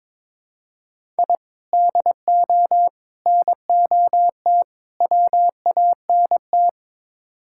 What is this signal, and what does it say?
Morse code, heard as a keyed tone, about 21 wpm, 700 Hz. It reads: I DO NOT WANT
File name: practice / 22wpm / Build A Phrase - 5-Word Phrases - 22wpm